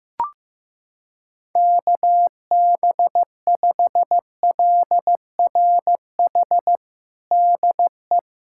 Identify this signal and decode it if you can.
Morse code, heard as a keyed tone, about 15 wpm, 700 Hz. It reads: KB5LRH DE
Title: USA Callsign DE Callsign 15wpm